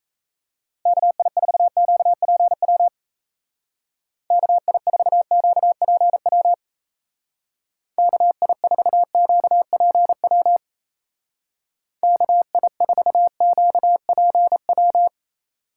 Morse code, heard as a keyed tone, about 34 wpm, 700 Hz. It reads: KI4QPW KI4QPW KI4QPW KI4QPW
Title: Speed Racing - USA Call Signs 28wpm